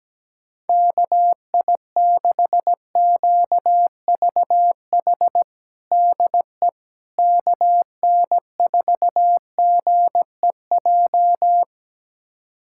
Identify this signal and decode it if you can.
Morse code, heard as a keyed tone, about 17 wpm, 700 Hz. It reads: KI6QVH DE KN4GEJ
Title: USA Callsign DE Callsign 17wpm